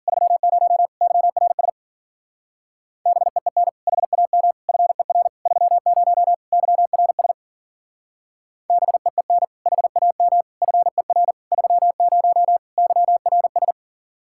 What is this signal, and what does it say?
Morse code, heard as a keyed tone, about 52 wpm, 700 Hz. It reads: 30 YRS BEEN HAM FER 30 YRS BEEN HAM FER 30 YRS